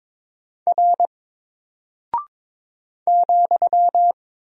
Morse code, heard as a keyed tone, about 22 wpm, 700 Hz. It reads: R ,
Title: Single Character (Letter:Number:Punctuation:Prosign) 22wpm